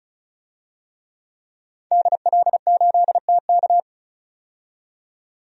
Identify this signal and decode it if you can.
Morse code, heard as a keyed tone, about 36 wpm, 700 Hz. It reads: DL8TK
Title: DXCC-100 Call Signs - 5 Characters - 35wpm